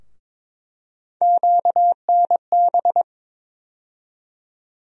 Morse code, heard as a keyed tone, about 22 wpm, 700 Hz. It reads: QNB
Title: USA Call Sign Postfixes - 3 Letters 22wpm